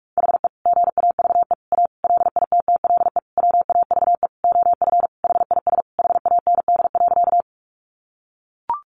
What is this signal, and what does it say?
Morse code, heard as a keyed tone, about 44 wpm, 700 Hz. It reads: HE GAVE A LITTLE WAVE OF HIS HAND.
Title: Sentences from Top 1400 Words 45wpm